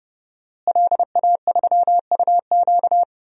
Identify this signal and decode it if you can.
Morse code, heard as a keyed tone, about 30 wpm, 700 Hz. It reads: LA3UQ
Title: DXCC-100 Call Signs - 5 Characters - Repeated 2x - 30wpm